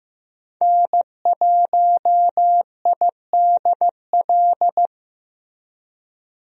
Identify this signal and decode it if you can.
Morse code, heard as a keyed tone, about 15 wpm, 700 Hz. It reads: N1IDL
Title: USA Call Signs - 5 Characters - Repeated 2x - 15wpm